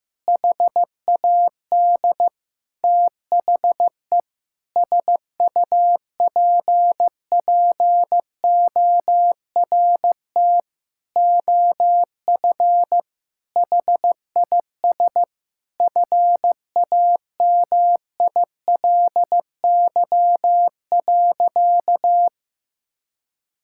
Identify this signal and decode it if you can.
Morse code, heard as a keyed tone, about 16 wpm, 700 Hz. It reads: HAD THE SUPPORT OF HIS FAMILY.
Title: Sentences from Top 1000 Words - Review Part 5 - 15wpm